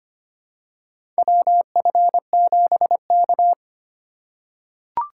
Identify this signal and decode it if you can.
Morse code, heard as a keyed tone, about 25 wpm, 700 Hz. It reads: WF7K
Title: USA Call Signs - 4 Characters - 8x Character Spacing - 25wpm